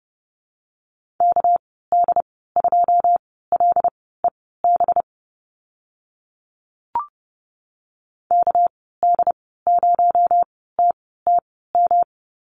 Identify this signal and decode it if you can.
Morse code, heard as a keyed tone, about 30 wpm, spaced as for 17 wpm, 700 Hz. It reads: KD2LEB KD0TTM
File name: Lesson 37g - Letter x - Call Signs 30wpm